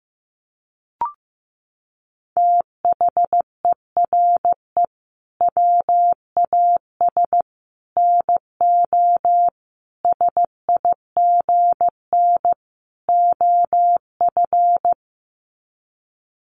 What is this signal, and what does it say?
Morse code, heard as a keyed tone, about 14 wpm, 700 Hz. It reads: THERE WAS NO SIGN OF
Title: Sets of 5 Words Encore2 15wpm